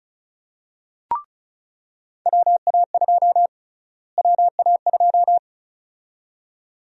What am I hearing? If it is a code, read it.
Morse code, heard as a keyed tone, about 34 wpm, 700 Hz. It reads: WA2 WA2
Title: USA Call Sign Prefixes - Repeated 2x - 35wpm